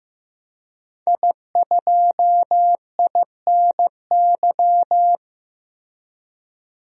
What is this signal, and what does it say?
Morse code, heard as a keyed tone, about 15 wpm, 700 Hz. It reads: I2INY
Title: DXCC-100 Call Signs - 5 Characters - Repeated 3x - 15wpm